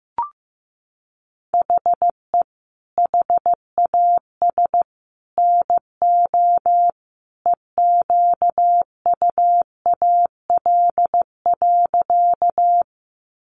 Morse code, heard as a keyed tone, about 15 wpm, 700 Hz. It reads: HE HAS NO EQUAL.
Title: Sentences from Top 1400 Words 15wpm